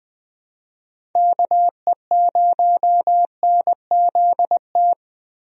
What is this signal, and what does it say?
Morse code, heard as a keyed tone, about 20 wpm, 700 Hz. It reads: KE0NZT
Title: USA Call Signs - 6 Characters - 20wpm